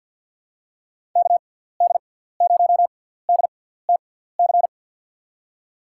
Morse code, heard as a keyed tone, about 53 wpm, spaced as for 17 wpm, 700 Hz. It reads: KD0DTC
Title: USA Call Signs - 6 Characters - 6x Character Spacing - 50wpm